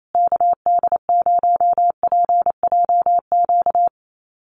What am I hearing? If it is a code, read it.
Morse code, heard as a keyed tone, about 28 wpm, 700 Hz. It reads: KD0PJQ
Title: USA Call Signs Repeated 2x 28wpm